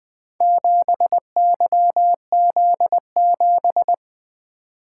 Morse code, heard as a keyed tone, about 20 wpm, 700 Hz. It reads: 7YZ7
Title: Four Letter-Number Combinations - 20wpm